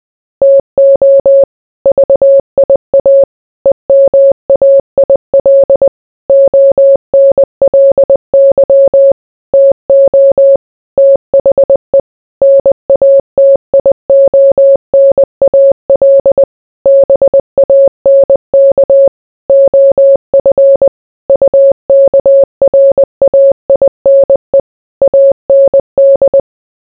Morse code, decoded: TO VIA EMAIL ONLY TO THE NATIONAL BANK OF UKRAINE AND

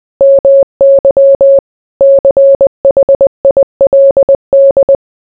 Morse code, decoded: MY CHILD

